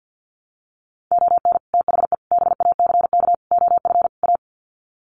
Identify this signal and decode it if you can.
Morse code, heard as a keyed tone, about 50 wpm, 700 Hz. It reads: ON THE BACK OF A
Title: Sets of 5 Words Encore2 50wpm